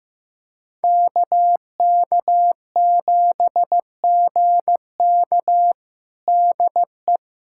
Morse code, heard as a keyed tone, about 15 wpm, 700 Hz. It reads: KK7GK DE
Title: USA Callsign DE Callsign - 5 Characters - 15wpm